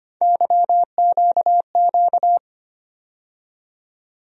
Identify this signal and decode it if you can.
Morse code, heard as a keyed tone, about 25 wpm, 700 Hz. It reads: YQQ
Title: USA Call Sign Postfixes - 3 Letters 25wpm